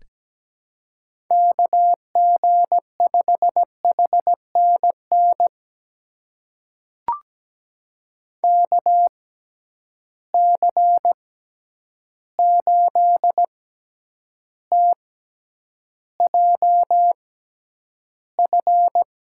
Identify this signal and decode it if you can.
Morse code, heard as a keyed tone, about 17 wpm, spaced as for 9 wpm, 700 Hz. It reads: KG5HNN KC8TJF
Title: USA Call Signs - 6 Characters - 6x Character Spacing - 17wpm